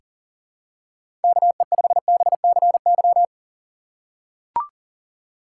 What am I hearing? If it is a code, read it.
Morse code, heard as a keyed tone, about 40 wpm, 700 Hz. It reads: KE5BCY
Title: K1USN SST QSO Practice - 40wpm